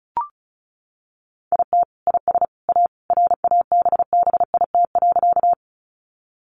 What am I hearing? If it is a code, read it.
Morse code, heard as a keyed tone, about 34 wpm, 700 Hz. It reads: IT IS A RABBIT.
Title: Easy Phrases 35wpm